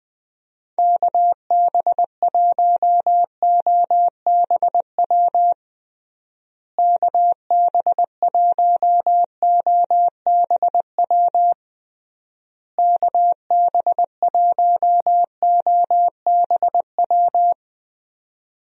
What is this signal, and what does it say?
Morse code, heard as a keyed tone, about 21 wpm, 700 Hz. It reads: KB1OBW KB1OBW KB1OBW
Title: USA Call Signs - 6 Characters - Repeated 3x - 20wpm